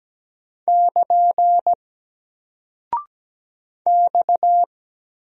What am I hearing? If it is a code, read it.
Morse code, heard as a keyed tone, about 17 wpm, 700 Hz. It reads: <KN> X
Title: Single Character (Letter:Number:Punctuation:Prosign) Alt 17wpm